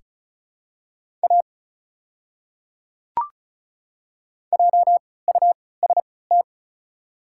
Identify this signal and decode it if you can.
Morse code, heard as a keyed tone, about 36 wpm, spaced as for 20 wpm, 700 Hz. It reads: A JUST
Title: 3x Character Spacing - Top 300 Words - 35wpm